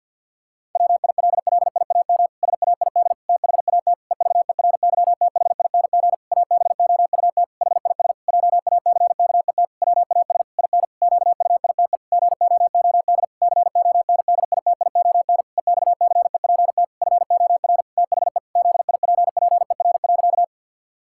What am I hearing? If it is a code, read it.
Morse code, heard as a keyed tone, about 52 wpm, 700 Hz. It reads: WILLIAM SAID THAT EVERYTHING ABOUT HIS JACKET WAS IN QUITE GOOD CONDITION EXCEPT FOR THE ZIPPER.
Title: Pangram Challenge v2 - 50wpm